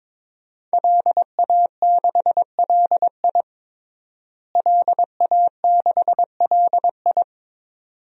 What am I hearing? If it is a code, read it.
Morse code, heard as a keyed tone, about 22 wpm, 700 Hz. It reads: LA6LI LA6LI
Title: DXCC-100 Call Signs - 5 Characters - Repeated 2x - 22wpm